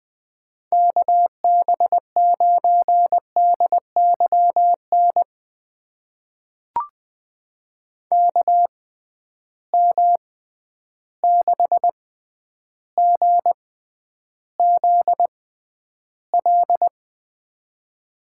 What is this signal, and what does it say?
Morse code, heard as a keyed tone, about 20 wpm, spaced as for 10 wpm, 700 Hz. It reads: KB9DYN KM6GZL
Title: USA Call Signs - 6 Characters - 6x Character Spacing - 20wpm